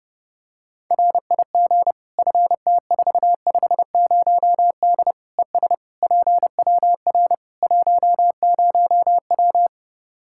Morse code, heard as a keyed tone, about 31 wpm, 700 Hz. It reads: RIG FT450D ES PWR 10W
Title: Practice QSO - Exchange 2 - Info - 30wpm